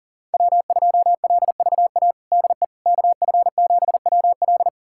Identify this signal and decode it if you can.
Morse code, heard as a keyed tone, about 40 wpm, 700 Hz. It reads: W2LVA DE KF7WL
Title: USA Callsign DE Callsign - 5 Characters - 40wpm